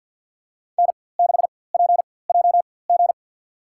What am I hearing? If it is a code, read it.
Morse code, heard as a keyed tone, about 46 wpm, spaced as for 23 wpm, 700 Hz. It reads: N6PJG
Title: USA Call Signs - 5 Characters - 4x Character Spacing - 50wpm